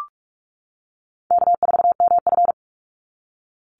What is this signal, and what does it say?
Morse code, heard as a keyed tone, about 44 wpm, 700 Hz. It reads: K4MF
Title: NA SOTA Chaser Callsigns 45wpm